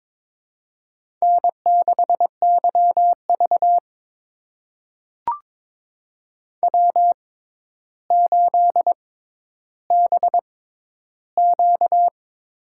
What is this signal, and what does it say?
Morse code, heard as a keyed tone, about 22 wpm, spaced as for 11 wpm, 700 Hz. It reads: N6YV W8BQ
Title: USA Call Signs - 4 Characters - 6x Character Spacing - 22wpm